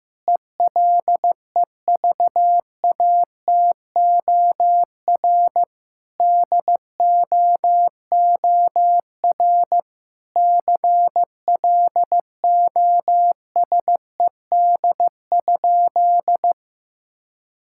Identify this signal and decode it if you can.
Morse code, heard as a keyed tone, about 16 wpm, 700 Hz. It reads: ELEVATOR DOOR CLOSED?
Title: Sentences from Top 1600 Words 15wpm